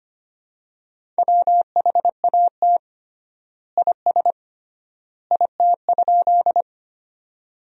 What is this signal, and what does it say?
Morse code, heard as a keyed tone, about 25 wpm, 700 Hz. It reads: WHAT IS IT?
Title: 3x Word-Spacing - Sentences from Top 100 Words - 25wpm